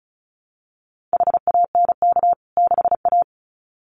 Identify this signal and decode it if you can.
Morse code, heard as a keyed tone, about 34 wpm, 700 Hz. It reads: HANK 6A
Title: CWT Exchange 35wpm